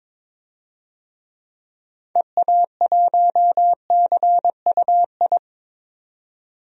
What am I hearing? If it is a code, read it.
Morse code, heard as a keyed tone, about 22 wpm, 700 Hz. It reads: EA1CUI
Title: DXCC-100 Call Signs - 6 Characters - 22wpm